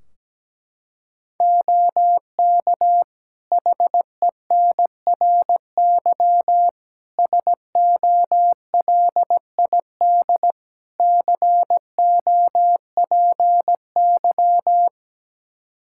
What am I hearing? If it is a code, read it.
Morse code, heard as a keyed tone, about 17 wpm, 700 Hz. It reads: OK HENRY SOLID COPY
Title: Practice QSO - Exchange 3 - More Info - 17wpm